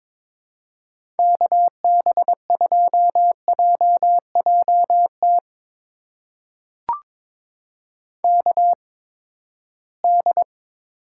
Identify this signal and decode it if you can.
Morse code, heard as a keyed tone, about 22 wpm, spaced as for 15 wpm, 700 Hz. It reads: KB2JJT KD
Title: USA Call Signs - 6 Characters - 8x Character Spacing - 22wpm